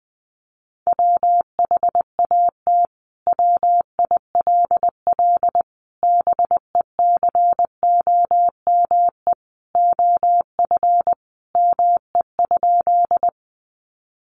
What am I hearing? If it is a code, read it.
Morse code, heard as a keyed tone, about 20 wpm, 700 Hz. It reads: WHAT WILL BECOME OF ME?
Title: Sentences from Top 600 Words 20wpm